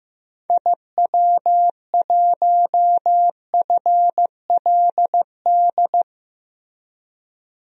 Morse code, heard as a keyed tone, about 15 wpm, 700 Hz. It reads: IW1FLD
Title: DXCC-100 Call Signs - 6 Characters - Repeated 3x - 15wpm